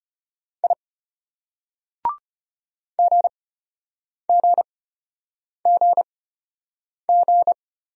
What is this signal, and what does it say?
Morse code, heard as a keyed tone, about 32 wpm, spaced as for 8 wpm, 700 Hz. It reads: I GGGG